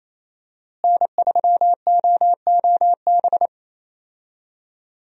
Morse code, heard as a keyed tone, about 28 wpm, 700 Hz. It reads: N3OOB
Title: USA Call Signs - 5 Characters - Repeated 3x - 28wpm